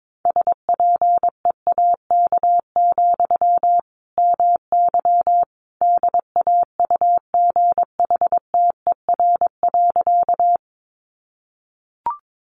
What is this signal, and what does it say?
Morse code, heard as a keyed tone, about 22 wpm, 700 Hz. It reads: SPEAK, MY DAUGHTER.
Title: Sentences from Top 1500 Words 22wpm